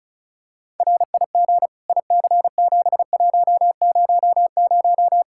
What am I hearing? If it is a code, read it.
Morse code, heard as a keyed tone, about 36 wpm, 700 Hz. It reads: RIG IC7100
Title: Practice QSO - Exchange 2 - Info - 35wpm